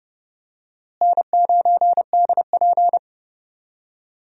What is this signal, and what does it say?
Morse code, heard as a keyed tone, about 32 wpm, 700 Hz. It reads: N9DP
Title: USA Call Signs - 4 Characters - Repeated 3x - 30wpm